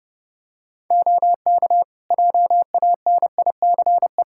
Morse code, heard as a keyed tone, about 30 wpm, 700 Hz. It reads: OK JANICE
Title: Practice QSO - Exchange 3 - More Info - 30wpm